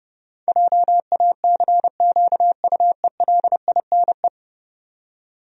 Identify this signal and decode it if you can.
Morse code, heard as a keyed tone, about 30 wpm, 700 Hz. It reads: JACQUELINE
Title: USA Common Names - 30wpm